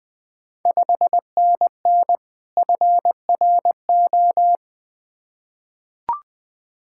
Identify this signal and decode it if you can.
Morse code, heard as a keyed tone, about 20 wpm, 690 Hz. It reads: 5NN FRO
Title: Ontario QSO Party Exchange 20wpm